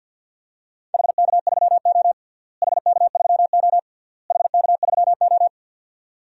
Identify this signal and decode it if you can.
Morse code, heard as a keyed tone, about 52 wpm, 700 Hz. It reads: HK3O HK3O HK3O
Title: DXCC-100 Call Signs - 4 Characters - Repeated 3x - 50wpm